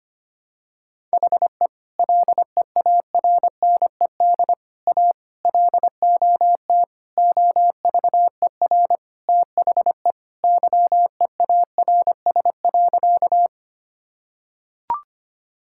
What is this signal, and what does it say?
Morse code, heard as a keyed tone, about 25 wpm, 700 Hz. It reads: HE LEARNED A LOT OVER THE YEARS.
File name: practice / 25wpm / Sentences with 3-Word Phrases - 25wpm